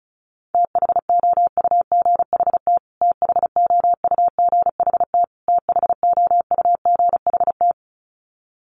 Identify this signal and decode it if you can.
Morse code, heard as a keyed tone, about 34 wpm, 700 Hz. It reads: THOUGHT THOUGHT THOUGHT
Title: Top 500 Words Repeated 35wpm